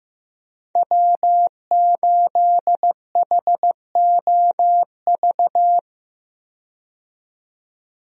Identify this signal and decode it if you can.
Morse code, heard as a keyed tone, about 15 wpm, 700 Hz. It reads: W8HOV
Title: USA Call Signs - 5 Characters - 15wpm